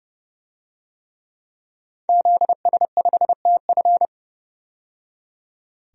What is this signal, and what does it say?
Morse code, heard as a keyed tone, about 31 wpm, 700 Hz. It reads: ZS5TF
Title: DXCC-100 Call Signs - 5 Characters - 30wpm